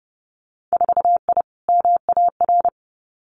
Morse code, heard as a keyed tone, about 30 wpm, 700 Hz. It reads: 4I MAR
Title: Winter Field Day Exchanges Encore 30wpm